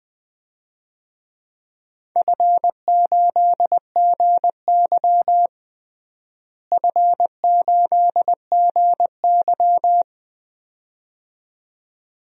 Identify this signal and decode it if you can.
Morse code, heard as a keyed tone, about 20 wpm, 700 Hz. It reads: F8GY F8GY
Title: DXCC-100 Call Signs - 4 Characters - Repeated 2x - 20wpm